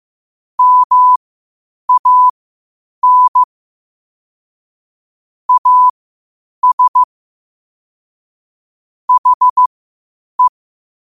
Morse code, heard as a keyed tone, about 15 wpm, spaced as for 8 wpm, 1000 Hz. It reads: MAN AS HE
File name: qotdfiles / 8wpm